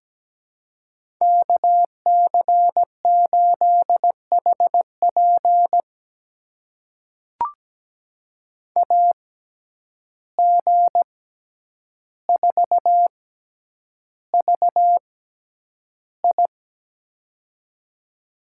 Morse code, heard as a keyed tone, about 17 wpm, spaced as for 9 wpm, 700 Hz. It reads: KC8HP AG4VI
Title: USA Call Signs - 5 Characters - 6x Character Spacing - 17wpm